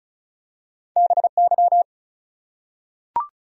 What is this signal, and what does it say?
Morse code, heard as a keyed tone, about 36 wpm, 700 Hz. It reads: BY
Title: Top 100 Words 35wpm